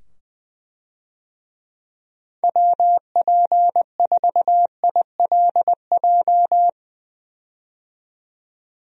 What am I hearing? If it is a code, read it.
Morse code, heard as a keyed tone, about 21 wpm, 700 Hz. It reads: WP4ILJ
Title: USA Call Signs - 6 Characters - 20wpm